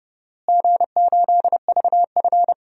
Morse code, heard as a keyed tone, about 32 wpm, 700 Hz. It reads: G8VF
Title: DXCC-100 Call Signs - 4 Characters - Repeated 2x - 30wpm